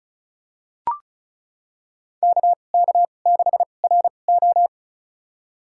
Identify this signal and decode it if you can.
Morse code, heard as a keyed tone, about 36 wpm, spaced as for 25 wpm, 700 Hz. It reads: KK6RO
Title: USA Call Signs - 5 Characters - 2x Character Spacing - 35wpm